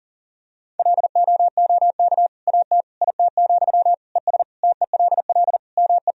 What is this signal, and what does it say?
Morse code, heard as a keyed tone, about 40 wpm, 700 Hz. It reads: LOOK AT IT, ES TELL ME
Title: Sentences from Top 200 Words and Abbreviations 40wpm